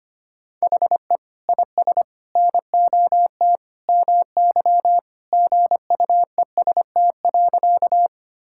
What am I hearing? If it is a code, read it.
Morse code, heard as a keyed tone, about 25 wpm, 700 Hz. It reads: HE IS NOT MY GUEST.
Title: Sentences from Top 1600 Words 25wpm